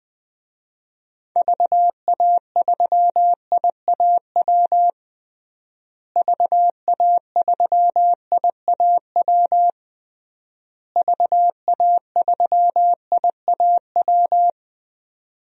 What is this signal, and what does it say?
Morse code, heard as a keyed tone, about 20 wpm, 700 Hz. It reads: VA3IAW VA3IAW VA3IAW
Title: DXCC-100 Call Signs - 6 Characters - Repeated 3x - 20wpm